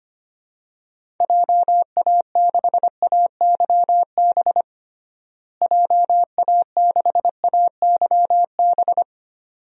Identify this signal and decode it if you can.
Morse code, heard as a keyed tone, about 25 wpm, 700 Hz. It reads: JA6AYB JA6AYB
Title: DXCC-100 Call Signs - 6 Characters - Repeated 2x - 25wpm